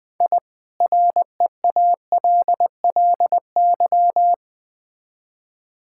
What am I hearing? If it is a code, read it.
Morse code, heard as a keyed tone, about 20 wpm, 700 Hz. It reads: I REALLY
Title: Build A Phrase - 4-Word Phrases - 20wpm